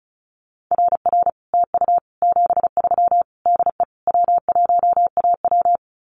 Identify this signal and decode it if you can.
Morse code, heard as a keyed tone, about 36 wpm, 700 Hz. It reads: RR TU 73 DE W1AW
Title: Common QSO Phrases Encore 35wpm